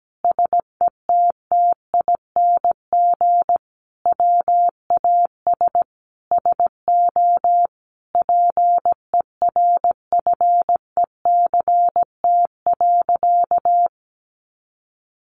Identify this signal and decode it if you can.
Morse code, heard as a keyed tone, about 17 wpm, 700 Hz. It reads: SETTING WAS SO PERFECT.
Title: Sentences from Top 1300 Words 17wpm